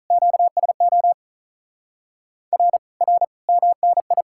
Morse code, heard as a keyed tone, about 48 wpm, 700 Hz. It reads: QSO R R MNI